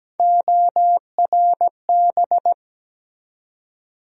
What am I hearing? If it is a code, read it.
Morse code, heard as a keyed tone, about 17 wpm, 700 Hz. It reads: ORB
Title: USA Call Sign Postfixes - 3 Letters 17wpm